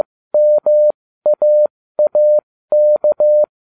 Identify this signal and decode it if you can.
Morse code, heard as a keyed tone, about 15 wpm, spaced as for 13 wpm, 600 Hz. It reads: MAAK